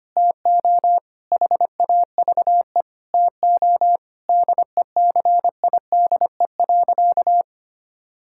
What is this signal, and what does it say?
Morse code, heard as a keyed tone, about 25 wpm, 700 Hz. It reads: TO HAVE TO DECIDE.
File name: practice / 25wpm / Sentences with 4-Word Phrases - 25wpm